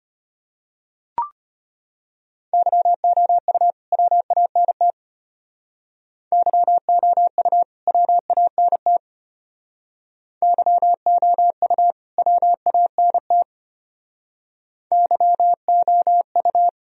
Morse code, heard as a keyed tone, about 32 wpm, 700 Hz. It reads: YOU WANT YOU WANT YOU WANT YOU